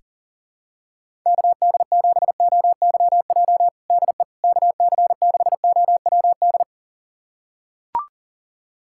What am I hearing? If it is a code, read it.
Morse code, heard as a keyed tone, about 40 wpm, 700 Hz. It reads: KD7OYJ DE KC6OWD
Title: USA Callsign DE Callsign - 6 Characters - 40wpm